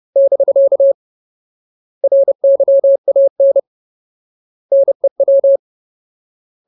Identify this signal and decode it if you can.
Morse code, heard as a keyed tone, about 30 wpm, 550 Hz. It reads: <BK> RYAN NEW
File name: Ultimate Word List / 30WPM Ultimate